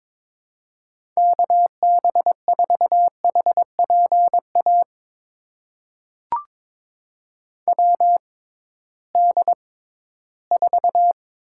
Morse code, heard as a keyed tone, about 22 wpm, spaced as for 14 wpm, 700 Hz. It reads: KB4HPA WD4
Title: USA Call Signs - 6 Characters - 6x Character Spacing - 22wpm